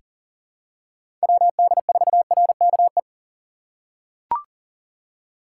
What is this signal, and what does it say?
Morse code, heard as a keyed tone, about 40 wpm, 700 Hz. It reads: WD4RKE
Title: USA Call Signs - 6 Characters - 2x Character Spacing - 40wpm